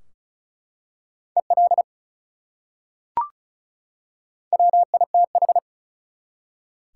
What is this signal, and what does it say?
Morse code, heard as a keyed tone, about 36 wpm, 700 Hz. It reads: EL WITH
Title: Top 180 QSO Elements 35wpm